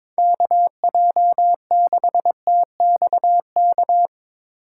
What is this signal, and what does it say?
Morse code, heard as a keyed tone, about 22 wpm, 700 Hz. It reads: KJ6TXK
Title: USA Call Signs - 6 Characters - 2x Character Spacing - 22wpm